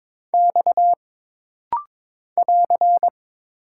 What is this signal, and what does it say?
Morse code, heard as a keyed tone, about 22 wpm, 700 Hz. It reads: X <AR>
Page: This rapid-fire timing is geared to push your ICR (Instant Character Recognition) skills.